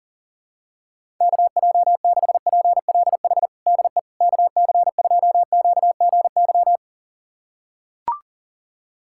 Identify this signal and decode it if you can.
Morse code, heard as a keyed tone, about 40 wpm, 700 Hz. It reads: KJ6PLH DE KC2QGY
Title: USA Callsign DE Callsign - 6 Characters - 40wpm